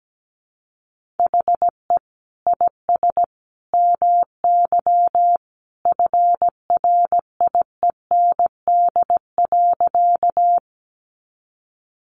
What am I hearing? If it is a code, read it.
Morse code, heard as a keyed tone, about 17 wpm, 700 Hz. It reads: HE IS MY FRIEND.
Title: Sentences from Top 300 Words 17wpm